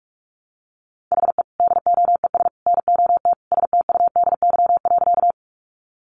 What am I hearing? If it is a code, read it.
Morse code, heard as a keyed tone, about 44 wpm, 700 Hz. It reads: HE DOES NOT STUDY.
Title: Additional Easy Phrases 45wpm